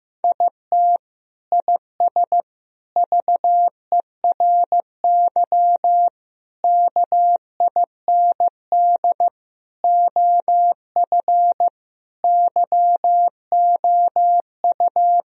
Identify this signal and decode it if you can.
Morse code, heard as a keyed tone, about 15 wpm, 700 Hz. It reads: IT IS VERY KIND OF YOU
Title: Sentences from Top 1000 Words - Review Part 1 - 15wpm